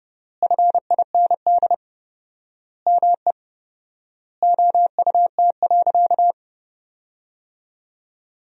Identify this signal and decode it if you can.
Morse code, heard as a keyed tone, about 32 wpm, 700 Hz. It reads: FIND ME OUT.
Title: 4x Word-Spacing - Sentences from Top 500 Words - 30wpm